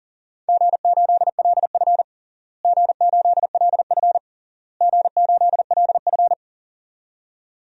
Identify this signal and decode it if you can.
Morse code, heard as a keyed tone, about 40 wpm, 700 Hz. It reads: G8LF G8LF G8LF
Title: DXCC-100 Call Signs - 4 Characters - Repeated 3x - 40wpm